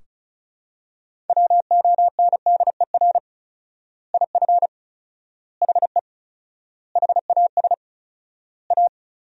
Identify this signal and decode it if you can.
Morse code, heard as a keyed tone, about 36 wpm, 700 Hz. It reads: WONDER IF HE HAS A
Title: 4x Word-Spacing - Sentences from Top 500 Words - 35wpm